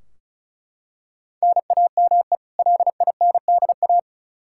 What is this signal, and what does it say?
Morse code, heard as a keyed tone, about 36 wpm, 700 Hz. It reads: NAME LINDA
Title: Practice QSO - Exchange 1 - Beginning - 35wpm